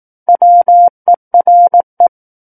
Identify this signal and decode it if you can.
Morse code, heard as a keyed tone, about 18 wpm, 700 Hz. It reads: WERE
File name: 18 WPM